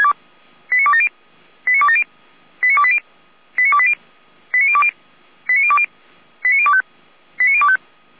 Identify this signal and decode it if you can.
Five-tone selective call (ZVEI), heard as quick runs of five tones, tones of about 70 ms. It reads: 89270 89270 89280 89280 89020 89020 89025 89025